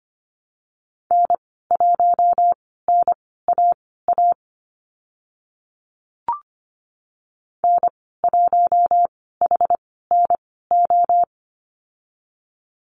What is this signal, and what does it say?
Morse code, heard as a keyed tone, about 25 wpm, spaced as for 16 wpm, 690 Hz. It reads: N1NAA N1HNO
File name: Lesson 15g - Number 5 - Call Signs 25wpm